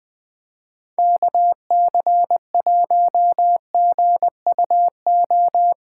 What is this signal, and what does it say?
Morse code, heard as a keyed tone, about 21 wpm, 700 Hz. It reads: KC1GUO
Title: USA Callsign DE Callsign 20wpm